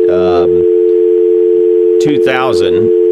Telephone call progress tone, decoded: dial tone